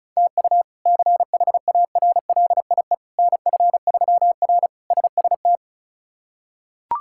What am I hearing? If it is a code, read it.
Morse code, heard as a keyed tone, about 34 wpm, 700 Hz. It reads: TU CHARLIE NF3R SST
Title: K1USN SST QSO Practice - 35wpm